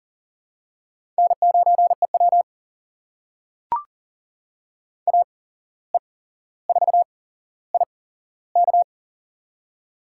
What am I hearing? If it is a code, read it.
Morse code, heard as a keyed tone, about 40 wpm, spaced as for 16 wpm, 700 Hz. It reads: N9EW AE4IK
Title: USA Call Signs - 8x Character Spacing - 40wpm